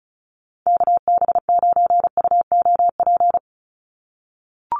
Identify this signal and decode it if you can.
Morse code, heard as a keyed tone, about 34 wpm, 700 Hz. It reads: KB9UOP
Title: USA Call Signs 35wpm